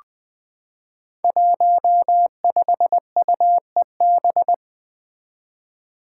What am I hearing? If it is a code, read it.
Morse code, heard as a keyed tone, about 21 wpm, 700 Hz. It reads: 15UEB
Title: Five Challenging Letter-Number Combinations 20wpm